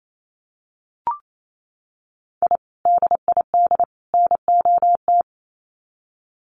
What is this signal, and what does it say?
Morse code, heard as a keyed tone, about 28 wpm, 700 Hz. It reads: I DID NOT
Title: Sets of 3 Words 28wpm